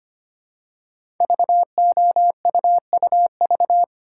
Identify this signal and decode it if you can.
Morse code, heard as a keyed tone, about 25 wpm, 700 Hz. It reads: VOUUV
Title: Code Groups - 5 Letters - 25wpm